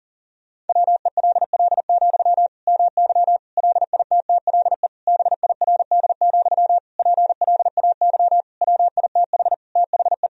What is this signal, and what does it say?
Morse code, heard as a keyed tone, about 40 wpm, 700 Hz. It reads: WELL, MY LITTLE BIRD, PLAY WITH THE
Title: Sentences from Top 1000 Words - Review Part 2 - 40wpm